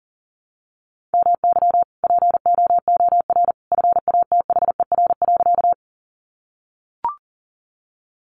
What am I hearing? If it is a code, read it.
Morse code, heard as a keyed tone, about 37 wpm, 700 Hz. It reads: MY POOR FATHER.
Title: Sentences from Top 1000 Words 40wpm